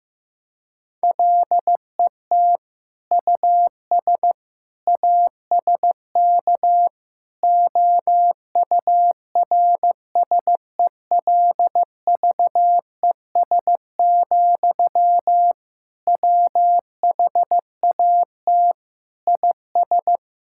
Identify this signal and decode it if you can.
Morse code, heard as a keyed tone, about 15 wpm, 700 Hz. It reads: LET US ASK OURSELVES, WHAT IS
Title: Sentences from Top 1600 Words 15wpm